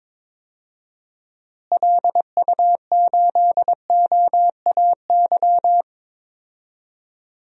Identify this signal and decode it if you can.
Morse code, heard as a keyed tone, about 22 wpm, 700 Hz. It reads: LU8OAY
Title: DXCC-100 Call Signs - 6 Characters - Repeated 3x - 22wpm